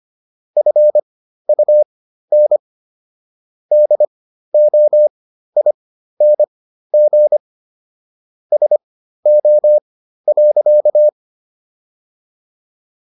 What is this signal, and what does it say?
Morse code, heard as a keyed tone, about 25 wpm, spaced as for 13 wpm, 600 Hz. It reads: FUN DOING SO.